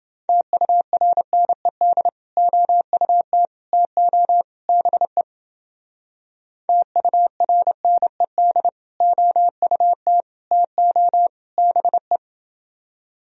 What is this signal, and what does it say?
Morse code, heard as a keyed tone, about 29 wpm, 700 Hz. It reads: TURNED OUT TO BE TURNED OUT TO BE